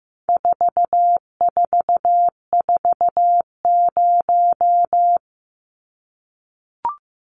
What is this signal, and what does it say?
Morse code, heard as a keyed tone, about 15 wpm, 700 Hz. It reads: 4440
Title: Numbers - 4-Digit 15wpm